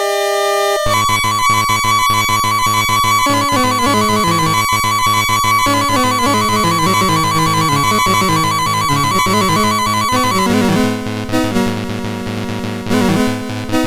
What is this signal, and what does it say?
There is no code.